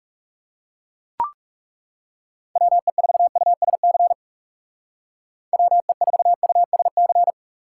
Morse code, heard as a keyed tone, about 36 wpm, 700 Hz. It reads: WE4USC WE4USC
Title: Speed Racing - USA 2x3 Call Signs 30wpm